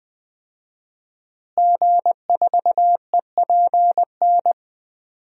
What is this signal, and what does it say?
Morse code, heard as a keyed tone, about 20 wpm, 700 Hz. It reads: G4EPN
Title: DXCC-100 Call Signs - 5 Characters - Repeated 2x - 20wpm